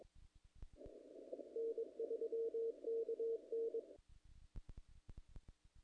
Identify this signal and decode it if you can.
Morse code, heard as a keyed tone, about 23 wpm, 470 Hz. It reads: N3KN
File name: contests / naqp-cw-k5tr-op-k5pi- -1800